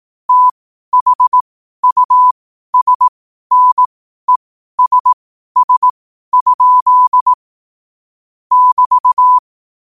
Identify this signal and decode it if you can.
Morse code, heard as a keyed tone, about 18 wpm, spaced as for 12 wpm, 1000 Hz. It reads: THUSNESS? <BT>
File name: qotdfiles / 12wpm